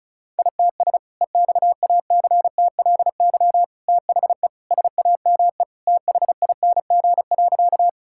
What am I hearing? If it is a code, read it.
Morse code, heard as a keyed tone, about 34 wpm, 700 Hz. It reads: ITS EXACTLY THE SAME THING.
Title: Sentences from Top 1000 Words - Review Part 6 - 35wpm